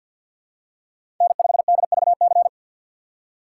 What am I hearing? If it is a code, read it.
Morse code, heard as a keyed tone, about 53 wpm, 700 Hz. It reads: N5DVC
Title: USA Call Signs - 5 Characters - 3x Character Spacing - 50wpm